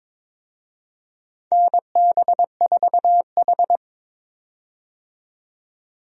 Morse code, heard as a keyed tone, about 23 wpm, 700 Hz. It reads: NB4H
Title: USA Call Signs - 4 Characters - 22wpm